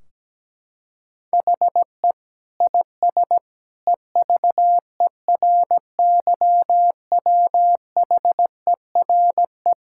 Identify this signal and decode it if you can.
Morse code, heard as a keyed tone, about 17 wpm, 700 Hz. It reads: HE IS EVERYWHERE
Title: Sentences from Top 1600 Words 17wpm